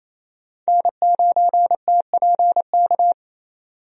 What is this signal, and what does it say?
Morse code, heard as a keyed tone, about 28 wpm, 700 Hz. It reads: N9TPK
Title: USA Call Signs - 5 Characters - 28wpm